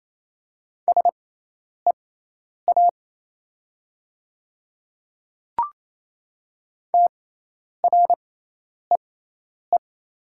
Morse code, heard as a keyed tone, about 27 wpm, spaced as for 10 wpm, 700 Hz. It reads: SEA TREE
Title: 6x Character Spacing - Top 300 Words - 28wpm